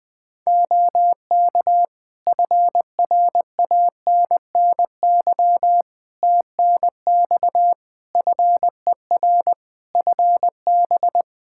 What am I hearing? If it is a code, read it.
Morse code, heard as a keyed tone, about 20 wpm, 700 Hz. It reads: OK FRANNY TNX FER FB
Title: Practice QSO - Exchange 4 - Ending - 20wpm